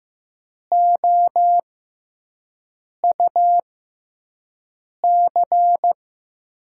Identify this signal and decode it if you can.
Morse code, heard as a keyed tone, about 15 wpm, spaced as for 5 wpm, 700 Hz. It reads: OUC
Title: USA Call Signs - 5 Characters - 6x Character Spacing - 15wpm